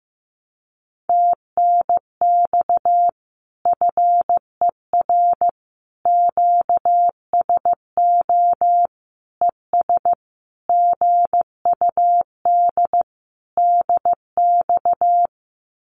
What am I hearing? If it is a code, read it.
Morse code, heard as a keyed tone, about 15 wpm, 700 Hz. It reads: TNX FER QSO ES GUD DX
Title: Common QSO Phrases Encore 15wpm